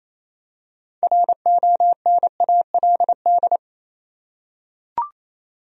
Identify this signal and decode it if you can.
Morse code, heard as a keyed tone, about 28 wpm, 700 Hz. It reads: RONALD
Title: USA Common Names - 28wpm